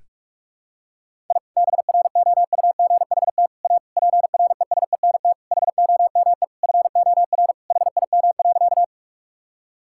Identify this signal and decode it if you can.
Morse code, heard as a keyed tone, about 46 wpm, 700 Hz. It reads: I BROUGHT A PRESENT HOME FOR HIM.
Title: Sentences from Top 1000 Words - Review Part 5 - 45wpm